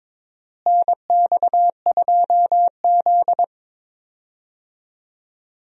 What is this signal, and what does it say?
Morse code, heard as a keyed tone, about 22 wpm, 700 Hz. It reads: NX2Z
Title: USA Call Signs - 4 Characters - Repeated 3x - 22wpm